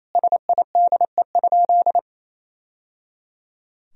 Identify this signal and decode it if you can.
Morse code, heard as a keyed tone, about 28 wpm, 700 Hz. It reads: SIDE?
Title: Sentences from Top 1000 Words - Review Part 3 - 28wpm